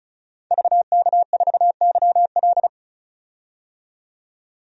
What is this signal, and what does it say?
Morse code, heard as a keyed tone, about 36 wpm, 700 Hz. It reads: VK4YL
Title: DXCC-100 Call Signs - 5 Characters - 35wpm